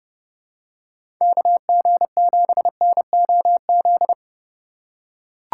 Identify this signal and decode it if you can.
Morse code, heard as a keyed tone, about 32 wpm, 700 Hz. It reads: KG7NOZ
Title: USA Call Signs - 6 Characters - 4x Character Spacing - 30wpm